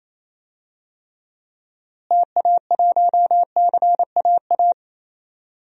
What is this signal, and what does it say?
Morse code, heard as a keyed tone, about 28 wpm, 700 Hz. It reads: TA1CAA
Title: DXCC-100 Call Signs - 6 Characters - Repeated 2x - 28wpm